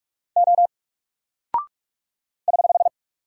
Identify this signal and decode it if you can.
Morse code, heard as a keyed tone, about 44 wpm, 700 Hz. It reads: O <HH>
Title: Single Character (Letter:Number:Punctuation:Prosign) Alt 45wpm